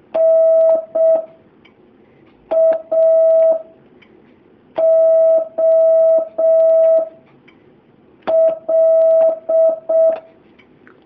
Morse code, heard as a keyed tone, about 6 wpm, 650 Hz. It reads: N A O L
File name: jh8cbh / 2019cw